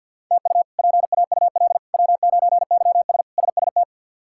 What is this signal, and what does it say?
Morse code, heard as a keyed tone, about 53 wpm, 700 Hz. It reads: TU PAUL W9YS SST
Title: K1USN SST QSO Practice - 50wpm